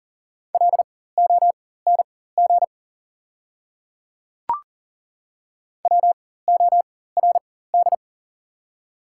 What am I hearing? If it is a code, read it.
Morse code, heard as a keyed tone, about 40 wpm, spaced as for 19 wpm, 700 Hz. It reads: LONG WORD
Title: 4x Character Spacing - Top 200 Words - 40wpm